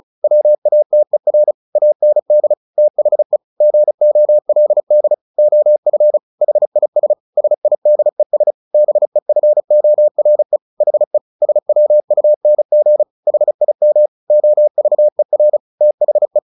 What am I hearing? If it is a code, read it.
Morse code, heard as a keyed tone, about 34 wpm, 600 Hz. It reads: WATER AND THE GOLD OF HIS SIDES BEFORE HE SWUNG HIM OVER THE
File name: the old man and the sea / the old man and the sea 1 35wpm